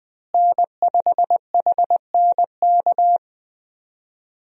Morse code, heard as a keyed tone, about 20 wpm, 700 Hz. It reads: N5HNK
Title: USA Call Signs - 5 Characters - Repeated 2x - 20wpm